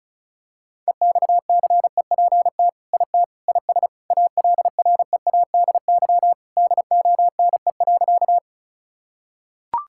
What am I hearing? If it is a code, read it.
Morse code, heard as a keyed tone, about 36 wpm, 700 Hz. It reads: EXCEPT IT IS ALREADY DONE.
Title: Sentences from Top 1600 Words 35wpm